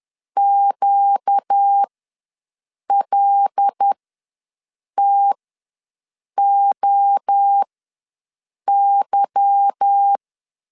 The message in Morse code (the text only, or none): QLTOY